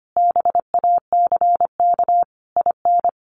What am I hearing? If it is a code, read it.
Morse code, heard as a keyed tone, about 25 wpm, 700 Hz. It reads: BACK IN
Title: Sentences from Top 1100 Words 25wpm